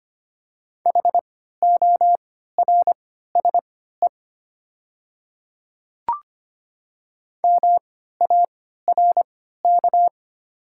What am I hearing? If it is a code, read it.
Morse code, heard as a keyed tone, about 25 wpm, spaced as for 14 wpm, 700 Hz. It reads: HORSE MARK
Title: 3x Character Spacing - Top 300 Words - 25wpm